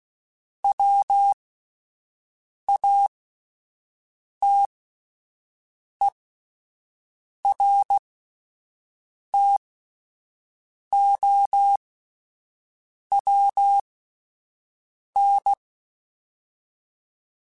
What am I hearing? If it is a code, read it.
Morse code, heard as a keyed tone, about 16 wpm, spaced as for 5 wpm, 800 Hz. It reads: WATERTOWN